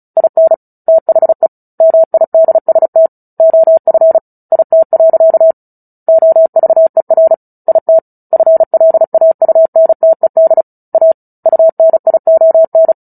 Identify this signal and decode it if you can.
Morse code, heard as a keyed tone, about 34 wpm, 650 Hz. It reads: IN THE MIDST OF IT. OVER IT FLAUNTED A UNION
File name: Book Two / 09-Chapter 9 - 35 WPM